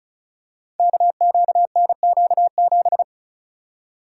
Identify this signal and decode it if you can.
Morse code, heard as a keyed tone, about 34 wpm, 700 Hz. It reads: KQNQ7
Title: Code Groups - 5 Letter:Numbers - 35wpm